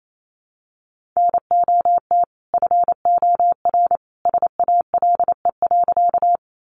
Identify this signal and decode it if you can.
Morse code, heard as a keyed tone, about 28 wpm, 700 Hz. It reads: NOT FOR SALE.
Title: Sentences from Top 1400 Words 28wpm